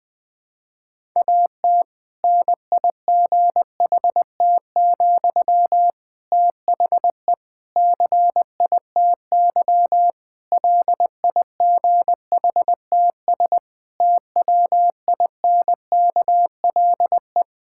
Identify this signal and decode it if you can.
Morse code, heard as a keyed tone, about 20 wpm, 700 Hz. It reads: AT NIGHT, THE CITY LIGHTS TWINKLE
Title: Difficult Sentences with Hints - 20wpm